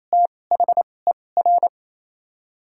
Morse code, heard as a keyed tone, about 28 wpm, spaced as for 20 wpm, 700 Hz. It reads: THER
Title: 2x Character Spacing - Top 200 Words - 28wpm